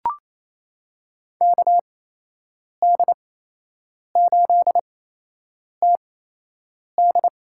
Morse code, heard as a keyed tone, about 28 wpm, spaced as for 8 wpm, 700 Hz. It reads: KD8TD
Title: USA Call Signs - 8x Character Spacing - 28wpm